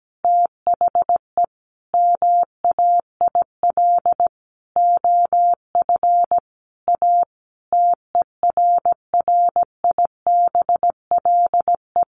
Morse code, heard as a keyed tone, about 17 wpm, 700 Hz. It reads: THE MAIL OF A TERRIBLE
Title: A Sound Of Thunder 17wpm